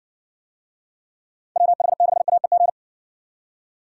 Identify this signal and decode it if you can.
Morse code, heard as a keyed tone, about 59 wpm, 700 Hz. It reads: WH6AEG